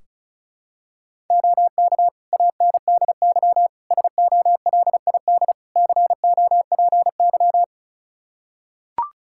Morse code, heard as a keyed tone, about 36 wpm, 700 Hz. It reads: OK ANDY SOLID COPY
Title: K1USN SST QSO Practice 35wpm